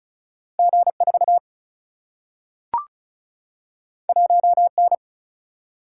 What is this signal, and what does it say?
Morse code, heard as a keyed tone, about 36 wpm, 700 Hz. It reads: G 4 1 N
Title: Single-Letter Single-Number 35wpm